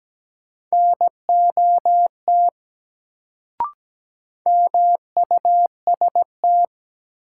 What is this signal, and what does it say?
Morse code, heard as a keyed tone, about 17 wpm, 700 Hz. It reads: NOT MUST
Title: Sets of 2 Words - New Word Drill - 17wpm